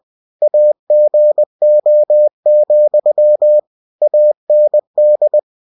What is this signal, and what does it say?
Morse code, heard as a keyed tone, about 20 wpm, 600 Hz. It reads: AGO, AND
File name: cw-books / winnie-the-pooh / winnie-20wpm-page-0027